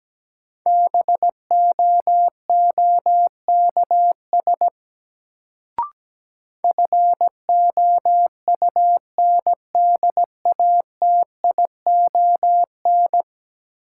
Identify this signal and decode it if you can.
Morse code, heard as a keyed tone, about 17 wpm, 700 Hz. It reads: BOOKS FOUNDATION
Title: Sentences from Top 1200 Words - New Word Drill Alt - 17wpm